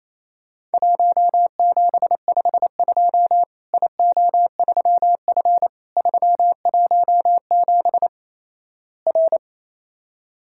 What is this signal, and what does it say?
Morse code, heard as a keyed tone, about 28 wpm, 700 Hz. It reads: 1752 IO3F 317 R
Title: Worked All Europe DX Contest - QTC Exchange - 28wpm